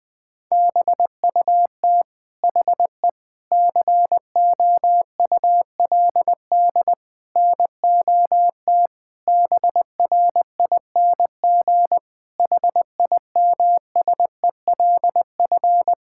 Morse code, heard as a keyed tone, about 21 wpm, 700 Hz. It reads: BUT HE COULD NOT BRING HIMSELF
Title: Sentences from Top 1600 Words 20wpm